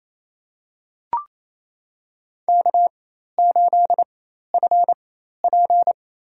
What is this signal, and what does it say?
Morse code, heard as a keyed tone, about 28 wpm, spaced as for 13 wpm, 700 Hz. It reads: K8FP
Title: USA Call Signs - 4 Characters - 4x Character Spacing - 28wpm